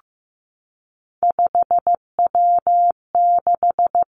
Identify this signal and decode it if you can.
Morse code, heard as a keyed tone, about 15 wpm, 700 Hz. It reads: 5W6
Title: Three Challenging Letter-Number Combinations 15wpm